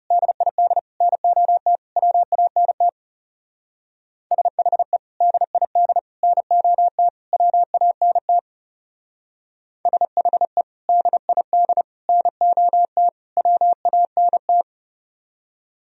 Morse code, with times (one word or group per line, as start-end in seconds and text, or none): 0.1-0.8 DID
1.0-1.8 NOT
2.0-2.9 WANT
4.3-5.0 SHE
5.2-6.0 DID
6.2-7.1 NOT
7.3-8.4 WANT
9.9-10.6 SHE
10.9-11.8 DID
12.1-13.1 NOT
13.4-14.6 WANT